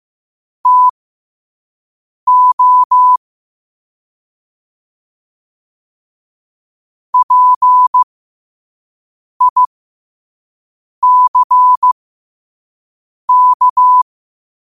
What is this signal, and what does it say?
Morse code, heard as a keyed tone, about 15 wpm, spaced as for 5 wpm, 1000 Hz. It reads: TO PICK
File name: qotdfiles / 5wpm